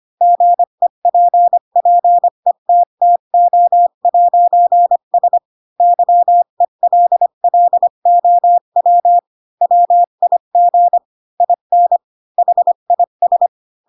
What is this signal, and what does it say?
Morse code, heard as a keyed tone, about 25 wpm, 700 Hz. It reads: GEPPETTO'S YELLOW WIG IN HIS